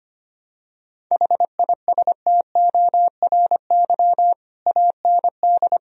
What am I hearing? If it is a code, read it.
Morse code, heard as a keyed tone, about 25 wpm, 700 Hz. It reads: HISTORY AND
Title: Sentences from Top 1000 Words - Review Part 2 - 25wpm